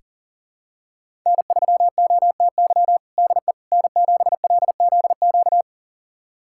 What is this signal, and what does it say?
Morse code, heard as a keyed tone, about 37 wpm, 700 Hz. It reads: N3OTY DE N7LZQ
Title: USA Callsign DE Callsign - 5 Characters - 40wpm